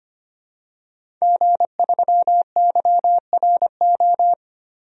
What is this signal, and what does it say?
Morse code, heard as a keyed tone, about 25 wpm, 700 Hz. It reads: G3YRO
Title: DXCC-100 Call Signs - 5 Characters - Repeated 3x - 25wpm